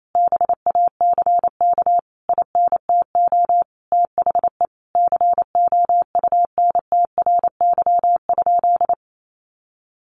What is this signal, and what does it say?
Morse code, heard as a keyed tone, about 28 wpm, 700 Hz. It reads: BACK INTO THE COUNTRY?
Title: Sentences from Top 1400 Words 28wpm